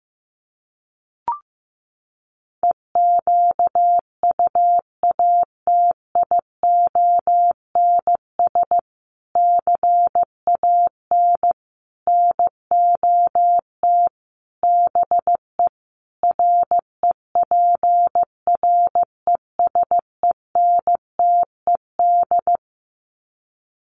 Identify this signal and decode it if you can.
Morse code, heard as a keyed tone, about 15 wpm, 700 Hz. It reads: EQUATIONS CAN NOT BE REPRESENTED
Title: Sets of 5 Words 15wpm